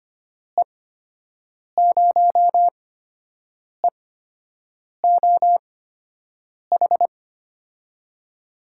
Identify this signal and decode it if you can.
Morse code, heard as a keyed tone, about 25 wpm, spaced as for 7 wpm, 700 Hz. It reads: E0EOH
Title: USA Call Signs - 6 Characters - 8x Character Spacing - 25wpm